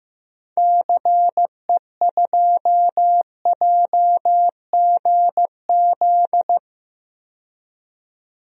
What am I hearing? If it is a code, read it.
Morse code, heard as a keyed tone, about 15 wpm, 700 Hz. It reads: CE2JGZ
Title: DXCC-100 Call Signs - 6 Characters - 15wpm